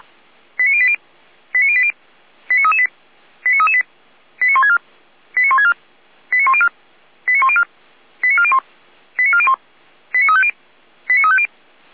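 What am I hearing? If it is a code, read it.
Five-tone selective call (ZVEI), heard as quick runs of five tones, tones of about 70 ms. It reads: 89080 89080 89338 89338 89164 89164 89104 89104 89401 89401 89350 89350
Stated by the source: ZVEI five-tone call received by SDR, from audio.f85360.feuerwehr-orlamuende.org (fire brigade Orlamünde).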